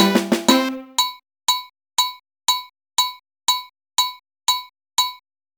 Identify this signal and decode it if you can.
There is no code.